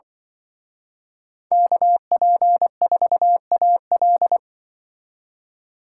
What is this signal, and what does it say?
Morse code, heard as a keyed tone, about 25 wpm, 700 Hz. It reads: KP4AL
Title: Speed Racing - USA Call Signs 20wpm